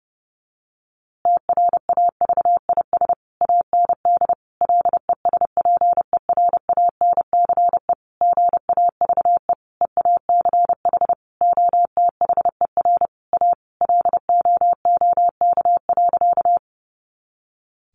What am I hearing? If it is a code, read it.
Morse code, heard as a keyed tone, about 30 wpm, 700 Hz. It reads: TRAVIS AND LESPERANCE GAVE EACH OTHER A LOOK.
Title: A Sound Of Thunder 30wpm